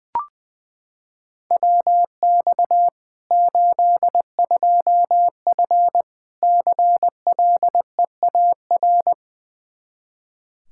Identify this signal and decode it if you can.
Morse code, heard as a keyed tone, about 20 wpm, 700 Hz. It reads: WX 82F CLEAR
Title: Practice QSO - Exchange 2 - Info - 20wpm